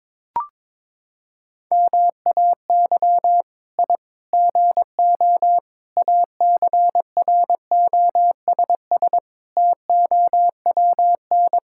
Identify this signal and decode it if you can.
Morse code, heard as a keyed tone, about 22 wpm, 700 Hz. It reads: MAY I GO ACROSS TOWN
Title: Sentences from Top 1000 Words - Review Part 5 - 22wpm